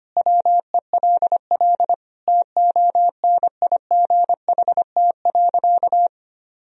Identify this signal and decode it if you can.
Morse code, heard as a keyed tone, about 25 wpm, 700 Hz. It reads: WELL TONIGHT.
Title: Sentences from Top 1000 Words - Review Part 3 - 25wpm